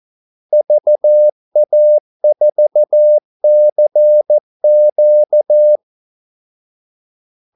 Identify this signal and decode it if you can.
Morse code, heard as a keyed tone, about 14 wpm, 600 Hz. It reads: VA4CQ
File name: Callsigns-14wpm